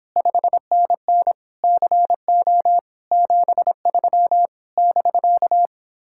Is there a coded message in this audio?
Yes: Morse code from 0.2 to 5.7 s, heard as a keyed tone, about 29 wpm, 700 Hz. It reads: 5NN CO 73 <BK>